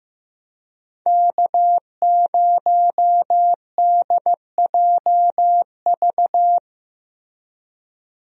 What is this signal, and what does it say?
Morse code, heard as a keyed tone, about 15 wpm, 700 Hz. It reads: K0DJV
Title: USA Call Signs - 5 Characters - Repeated 2x - 15wpm